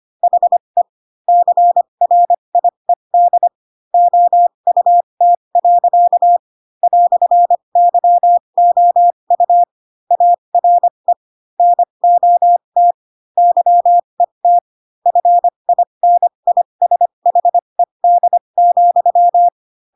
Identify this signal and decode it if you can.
Morse code, heard as a keyed tone, about 25 wpm, spaced as for 22 wpm, 700 Hz. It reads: HE CRIED OUT. "YOU ARE NOT YET FINISHED,
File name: Pinocchio / Pinocchio ch3 18 25 ik1hge